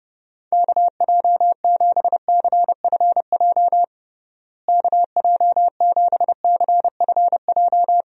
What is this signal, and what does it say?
Morse code, heard as a keyed tone, about 32 wpm, 700 Hz. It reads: KJ7CFJ KJ7CFJ
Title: USA Call Signs - 6 Characters - Repeated 2x - 30wpm